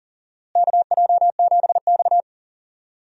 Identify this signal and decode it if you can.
Morse code, heard as a keyed tone, about 40 wpm, 700 Hz. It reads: KJ7X
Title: USA Call Signs - 4 Characters - Repeated 3x - 40wpm